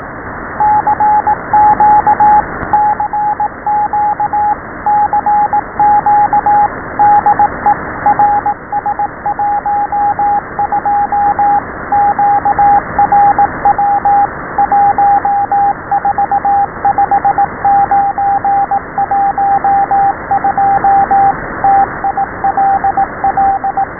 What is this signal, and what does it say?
Morse code, heard as a keyed tone, about 18 wpm, 780 Hz, with radio noise. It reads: CQ CQ CQ DE RS12 QRW 145912 TILL
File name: SBFFT12 / RS-12.WAV / RS-12